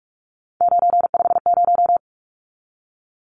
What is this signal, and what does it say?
Morse code, heard as a keyed tone, about 44 wpm, 700 Hz. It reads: 950
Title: Numbers - 3-Digit 45wpm